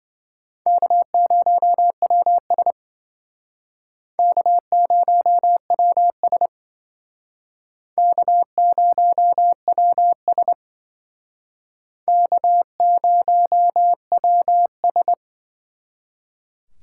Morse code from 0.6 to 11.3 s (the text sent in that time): K0WS K0WS K0WS